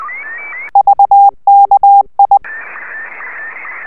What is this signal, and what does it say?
Morse code, heard as a keyed tone, about 20 wpm, 800 Hz. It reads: VKI
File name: sstv / M1-ZL2TJM